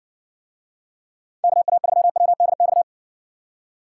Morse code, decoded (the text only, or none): KA3WDX